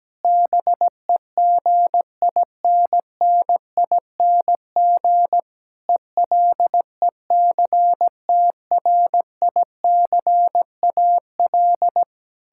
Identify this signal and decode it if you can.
Morse code, heard as a keyed tone, about 17 wpm, 700 Hz. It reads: BEGINNING ELECTRICAL
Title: Sentences from Top 1000 Words - Review Part 6 - 17wpm